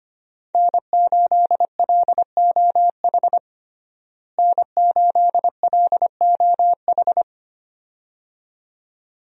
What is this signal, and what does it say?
Morse code, heard as a keyed tone, about 25 wpm, 690 Hz. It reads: N8LOH N8LOH
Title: USA Call Signs - 5 Characters - Repeated 2x - 25wpm